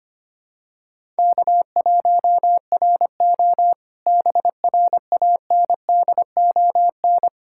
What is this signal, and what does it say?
Morse code, heard as a keyed tone, about 25 wpm, 700 Hz. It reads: K1RO BRANDON
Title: K1USN SST QSO Practice - 25wpm